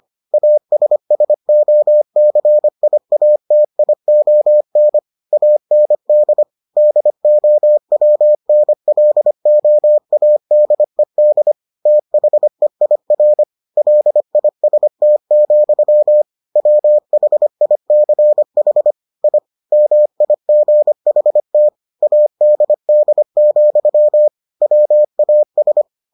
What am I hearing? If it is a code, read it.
Morse code, heard as a keyed tone, about 25 wpm, 600 Hz. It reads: ASSOCIATION AND DOWNLOADED THEIR LIST, WHICH I MIGHT ADD, WAS